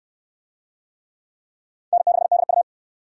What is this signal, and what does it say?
Morse code, heard as a keyed tone, about 69 wpm, 700 Hz. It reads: N6DU